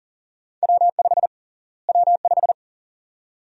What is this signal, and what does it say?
Morse code, heard as a keyed tone, about 40 wpm, 700 Hz. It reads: W5 W5
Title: USA Call Sign Prefixes - Repeated 2x - 40wpm